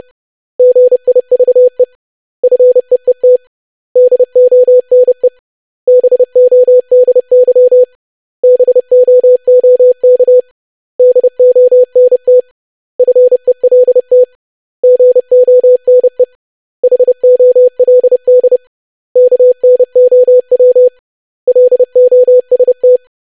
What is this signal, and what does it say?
Morse code, heard as a keyed tone, about 30 wpm, 500 Hz. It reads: GIVE FEET DONE BODY BOOK DONT FELT GONE HOLD KNOW LOST
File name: lesson / web / basic500-25wpm